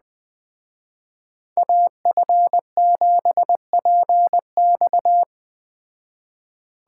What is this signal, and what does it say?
Morse code, heard as a keyed tone, about 20 wpm, 700 Hz. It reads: AF7PX
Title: Speed Racing - USA 2x2 Call Signs 17wpm